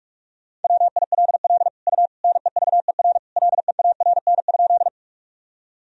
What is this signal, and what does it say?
Morse code, heard as a keyed tone, about 41 wpm, 700 Hz. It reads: WILL U NEVER LEARN?
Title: Sentences from Top 300 Words and Abbreviations 45wpm